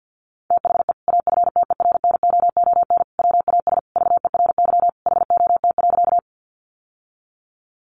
Morse code, heard as a keyed tone, about 50 wpm, 700 Hz. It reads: THE AFTERNOON WAS VERY HOT.
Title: Sentences from Top 1500 Words 50wpm